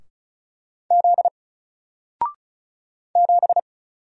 Morse code, heard as a keyed tone, about 36 wpm, 700 Hz. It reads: Z 7
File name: Single-Letter-Number-35wpm